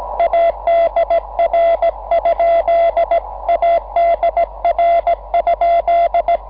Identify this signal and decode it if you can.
Morse code, heard as a keyed tone, about 17 wpm, 690 Hz. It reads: ADR? ADR?
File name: 6M audio / vu2rm gi0otc cw cycle23